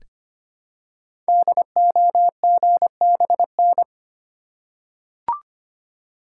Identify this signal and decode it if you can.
Morse code, heard as a keyed tone, about 25 wpm, 700 Hz. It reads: DOGBN
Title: Code Groups - 5 Letters - 25wpm